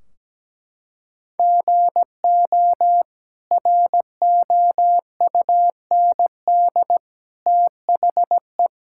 Morse code, heard as a keyed tone, about 17 wpm, 700 Hz. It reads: GO ROUND THE
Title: Sentences from Top 1600 Words 17wpm